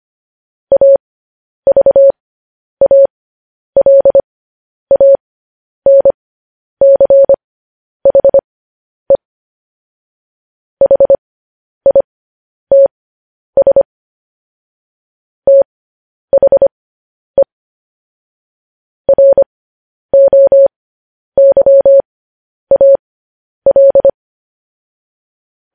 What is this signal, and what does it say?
Morse code, heard as a keyed tone, about 25 wpm, spaced as for 10 wpm, 550 Hz. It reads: AVALANCHE HITS THE ROYAL
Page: news-headlines-in-morse-code-at-10-wpm